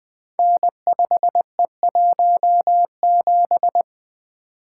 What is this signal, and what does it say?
Morse code, heard as a keyed tone, about 20 wpm, 700 Hz. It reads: N5E17
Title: Code Groups - 5 Letter:Numbers - 20wpm